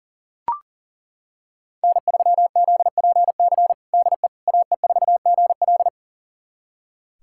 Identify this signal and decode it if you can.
Morse code, heard as a keyed tone, about 40 wpm, 700 Hz. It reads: N3ZPC DE AE4GL
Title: USA Callsign DE Callsign - 5 Characters - 40wpm